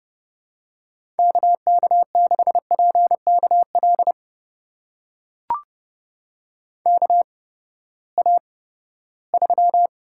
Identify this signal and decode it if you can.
Morse code, heard as a keyed tone, about 30 wpm, spaced as for 17 wpm, 700 Hz. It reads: KK6PKL KA3
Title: USA Call Signs - 6 Characters - 8x Character Spacing - 30wpm